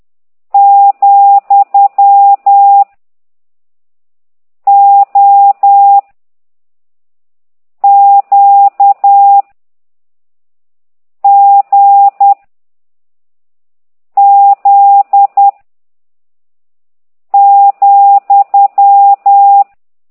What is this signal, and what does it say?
Morse code, heard as a keyed tone, about 10 wpm, 800 Hz. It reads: , O Q G Z ,